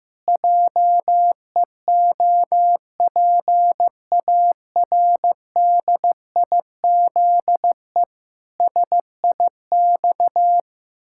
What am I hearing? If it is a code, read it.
Morse code, heard as a keyed tone, about 15 wpm, 700 Hz. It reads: JEOPARDIZE SIX
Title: Pangram Challenge v2 - 15wpm